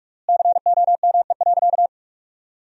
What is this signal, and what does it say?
Morse code, heard as a keyed tone, about 46 wpm, 700 Hz. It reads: COME.
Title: 4x Word-Spacing - Sentences from Top 100 Words - 45wpm